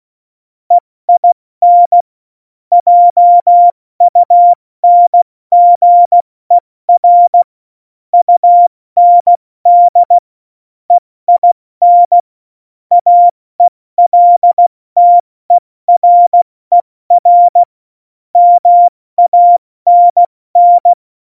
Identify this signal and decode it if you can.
Morse code, heard as a keyed tone, about 16 wpm, 700 Hz. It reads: EIN JUNGER UND EIN AELTERER MANN
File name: Morsekurs / Witze / Witz 18. Juni 2019, 70 F80